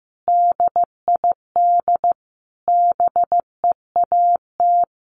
Morse code, heard as a keyed tone, about 15 wpm, 700 Hz. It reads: DID BEAT
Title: Sentences from Top 1300 Words 15wpm